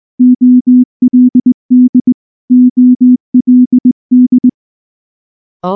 Morse code, heard as a keyed tone, about 19 wpm, 250 Hz. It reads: OLD OLD
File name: morse / morse-training-20WPM-6